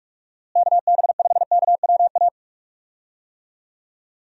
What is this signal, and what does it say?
Morse code, heard as a keyed tone, about 47 wpm, 700 Hz. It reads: KB5KWA
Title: USA Call Signs - 6 Characters - Repeated 3x - 45wpm